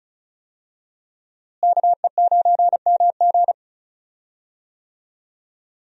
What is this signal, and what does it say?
Morse code, heard as a keyed tone, about 36 wpm, 700 Hz. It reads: KE9MG
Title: USA Call Signs - 5 Characters - Repeated 3x - 35wpm